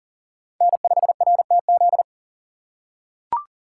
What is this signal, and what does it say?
Morse code, heard as a keyed tone, about 40 wpm, 700 Hz. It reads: N5RTZ
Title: USA Call Signs - 5 Characters - 6x Character Spacing - 40wpm